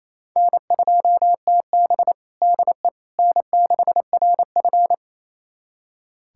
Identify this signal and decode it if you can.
Morse code, heard as a keyed tone, about 28 wpm, 700 Hz. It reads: N2TB DE N6RF
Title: USA Callsign DE Callsign - 4 Characters - 28wpm